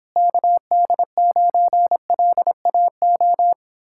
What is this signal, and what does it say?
Morse code, heard as a keyed tone, about 29 wpm, 700 Hz. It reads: KD9LAO